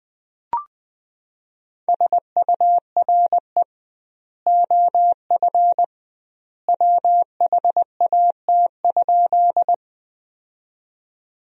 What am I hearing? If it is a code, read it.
Morse code, heard as a keyed tone, about 20 wpm, 700 Hz. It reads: SURE OF WHAT?
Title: 2x Word-Spacing - Sentences from Top 300 Words - 20wpm